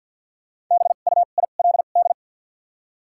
Morse code, heard as a keyed tone, about 53 wpm, 700 Hz. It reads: B U I L D
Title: 2x Character Spacing - Top 300 Words - 50wpm